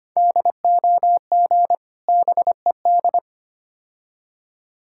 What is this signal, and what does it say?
Morse code, heard as a keyed tone, about 25 wpm, 700 Hz. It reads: DOG BED
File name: practice / 25wpm / Difficult Words with Hints - 25wpm